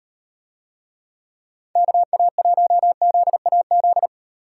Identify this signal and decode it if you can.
Morse code, heard as a keyed tone, about 38 wpm, 700 Hz. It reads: KA1ZAZ